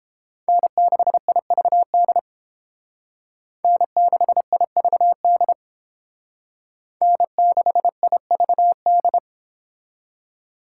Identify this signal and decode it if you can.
Morse code, heard as a keyed tone, about 30 wpm, 700 Hz. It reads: N6IVD N6IVD N6IVD